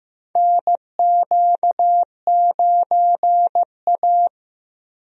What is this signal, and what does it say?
Morse code, heard as a keyed tone, about 15 wpm, 700 Hz. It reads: NQ9A
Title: USA Call Signs - 4 Characters - Repeated 2x - 15wpm